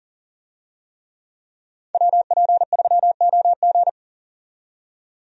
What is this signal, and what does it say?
Morse code, heard as a keyed tone, about 37 wpm, 700 Hz. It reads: WP3OG
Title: USA Call Signs - 5 Characters - 40wpm